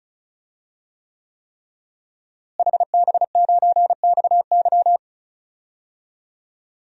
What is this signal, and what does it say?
Morse code, heard as a keyed tone, about 34 wpm, 700 Hz. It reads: HB9XY
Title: DXCC-100 Call Signs - 5 Characters - Repeated 3x - 35wpm